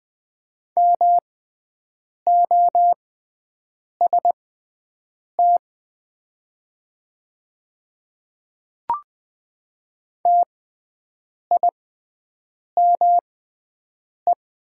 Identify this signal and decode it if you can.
Morse code, heard as a keyed tone, about 20 wpm, spaced as for 7 wpm, 700 Hz. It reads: MOST TIME
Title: 6x Character Spacing - Top 100 Words - 20wpm